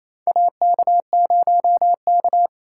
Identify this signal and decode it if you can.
Morse code, heard as a keyed tone, about 28 wpm, 700 Hz. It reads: AK0K
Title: USA Call Signs - 4 Characters - 3x Character Spacing - 28wpm